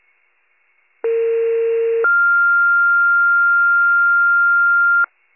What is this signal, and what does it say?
Two-tone page, heard as a one-second tone, then a three-second tone, about 460 then 1400 Hz. It is heard from 1.0 to 5.0 s.